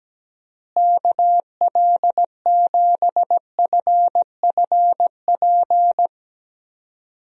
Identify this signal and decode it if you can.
Morse code, heard as a keyed tone, about 17 wpm, 700 Hz. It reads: KL7FFP
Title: USA Call Signs - 6 Characters - Repeated 3x - 17wpm